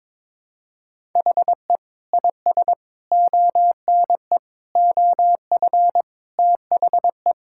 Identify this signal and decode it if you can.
Morse code, heard as a keyed tone, about 22 wpm, 700 Hz. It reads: HE IS ONE OF THE
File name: practice / 22wpm / Sentences with 4-Word Phrases - 22wpm